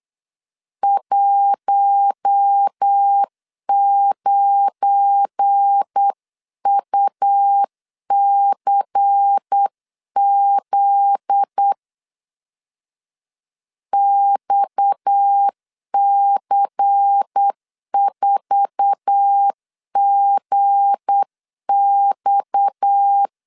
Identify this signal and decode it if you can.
Morse code, heard as a keyed tone, about 8 wpm, 800 Hz. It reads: 19UCZ XC4GX
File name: CW / DET / 1 9 1500C S A 8wpm 5637